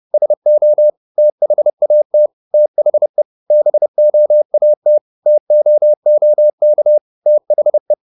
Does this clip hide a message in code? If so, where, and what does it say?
Yes: Morse code from 0.1 to 7.9 s, heard as a keyed tone, about 30 wpm, 600 Hz. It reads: SO THAT THE BOAT TOOK THE